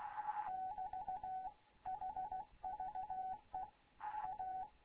Morse code, heard as a keyed tone, about 16 wpm, 750 Hz, with radio noise. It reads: <BT> HVE A
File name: Morze-HA2MN / hangp / 2 11 GYAKORLAT